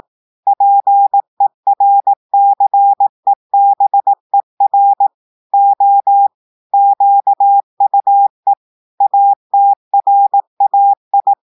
Morse code, heard as a keyed tone, about 18 wpm, 800 Hz. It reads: PERCEBER O QUE ATRAI
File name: portugais texte en clair 2 vitesse 18 mots minutes